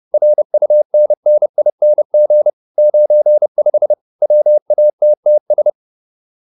Morse code, heard as a keyed tone, about 30 wpm, 600 Hz. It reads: RUNNING 95 WATTS